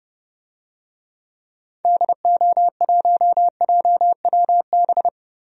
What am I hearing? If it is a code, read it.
Morse code, heard as a keyed tone, about 30 wpm, 700 Hz. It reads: DO1JWB
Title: DXCC-100 Call Signs - 6 Characters - 30wpm